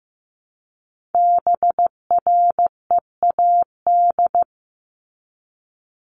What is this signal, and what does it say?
Morse code, heard as a keyed tone, about 15 wpm, 700 Hz. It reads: BREAD
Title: Top 1501-1750 Words 15wpm